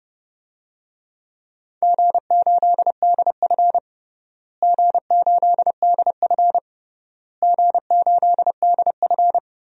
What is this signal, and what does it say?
Morse code, heard as a keyed tone, about 32 wpm, 700 Hz. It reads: G8DF G8DF G8DF
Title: DXCC-100 Call Signs - 4 Characters - Repeated 3x - 30wpm